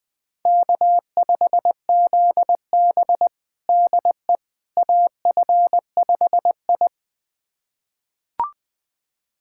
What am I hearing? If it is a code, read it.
Morse code, heard as a keyed tone, about 20 wpm, 700 Hz. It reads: K5ZB DE AF5I
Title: USA Callsign DE Callsign - 4 Characters - 20wpm